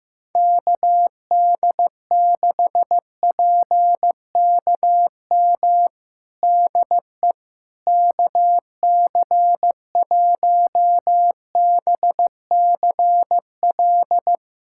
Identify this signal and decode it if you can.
Morse code, heard as a keyed tone, about 15 wpm, 700 Hz. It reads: KD6PKM DE KC1BCL
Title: USA Callsign DE Callsign 15wpm